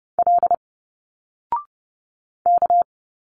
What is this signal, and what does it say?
Morse code, heard as a keyed tone, about 30 wpm, 700 Hz. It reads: L K
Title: Single Letters 30wpm